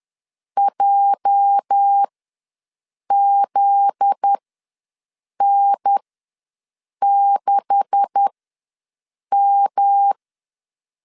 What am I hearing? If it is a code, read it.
Morse code, heard as a keyed tone, about 10 wpm, spaced as for 6 wpm, 800 Hz. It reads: JZN6M
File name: LEZIONI / 10 SIGNS / 10A SLASH ws 10wpm 4121